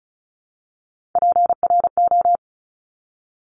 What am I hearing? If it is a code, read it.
Morse code, heard as a keyed tone, about 36 wpm, 700 Hz. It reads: PRO
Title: Top Three Letter Combinations in Words 35wpm